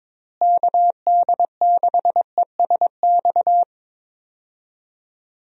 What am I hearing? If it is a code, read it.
Morse code, heard as a keyed tone, about 21 wpm, 700 Hz. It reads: KD6ESX
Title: USA Call Signs - 6 Characters - Repeated 2x - 22wpm